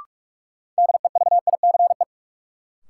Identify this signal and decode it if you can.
Morse code, heard as a keyed tone, about 46 wpm, 700 Hz. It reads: DEVICE
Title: Sentences from Top 1500 Words - New Word Drill - 45wpm